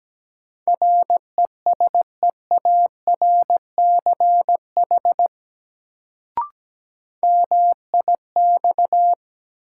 Morse code, heard as a keyed tone, about 17 wpm, 700 Hz. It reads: RESEARCH MIX
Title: Sentences from Top 1000 Words - Review Part 4 - New Word Drill - 17wpm